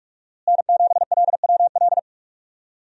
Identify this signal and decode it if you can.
Morse code, heard as a keyed tone, about 45 wpm, 700 Hz. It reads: N7LWL
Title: USA Call Signs - 5 Characters - Repeated 3x - 45wpm